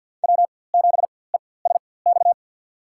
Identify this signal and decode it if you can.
Morse code, heard as a keyed tone, about 49 wpm, spaced as for 23 wpm, 700 Hz. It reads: W7ESX
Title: USA Call Signs - 5 Characters - 4x Character Spacing - 50wpm